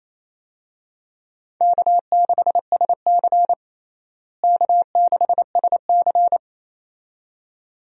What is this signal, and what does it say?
Morse code, heard as a keyed tone, about 28 wpm, 700 Hz. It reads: K6SC K6SC
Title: USA Call Signs - 4 Characters - Repeated 2x - 28wpm